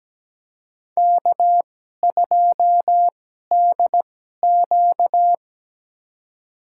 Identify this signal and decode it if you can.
Morse code, heard as a keyed tone, about 17 wpm, spaced as for 12 wpm, 700 Hz. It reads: K2DQ
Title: USA Call Signs - 4 Characters - 2x Character Spacing - 17wpm